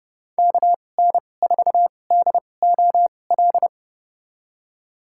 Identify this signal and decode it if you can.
Morse code, heard as a keyed tone, about 30 wpm, spaced as for 22 wpm, 700 Hz. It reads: KN4DOL
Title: USA Call Signs - 2x Character Spacing - 30wpm